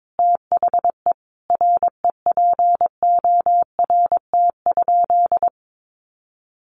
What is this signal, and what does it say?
Morse code, heard as a keyed tone, about 22 wpm, 700 Hz. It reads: THE REPORT?
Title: Sentences from Top 1200 Words 22wpm